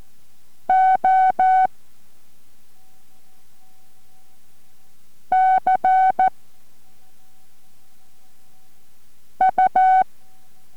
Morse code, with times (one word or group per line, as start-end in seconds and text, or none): 0.7-1.7 O
5.3-6.3 C
9.4-10.0 U